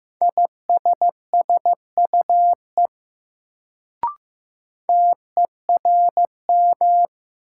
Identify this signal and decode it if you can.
Morse code, heard as a keyed tone, about 15 wpm, 700 Hz. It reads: ISSUE TERM
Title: Sentences from Top 900 Words - New Word Drill - 15wpm